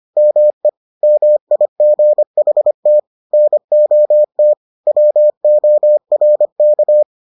Morse code, heard as a keyed tone, about 25 wpm, 600 Hz. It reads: ME MIGHT NOT WORK